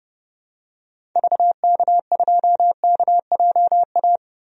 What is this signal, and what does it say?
Morse code, heard as a keyed tone, about 30 wpm, 700 Hz. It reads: VK2KJA
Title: DXCC-100 Call Signs - 6 Characters - 30wpm